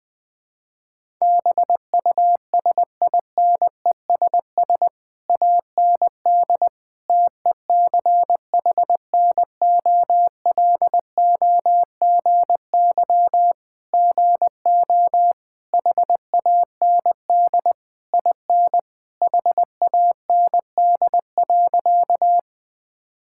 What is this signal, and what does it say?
Morse code, heard as a keyed tone, about 20 wpm, 700 Hz. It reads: BUSINESS AND TECHNOLOGY GO HAND IN HAND.
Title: Sentences from Top 1000 Words - Review Part 5 - 20wpm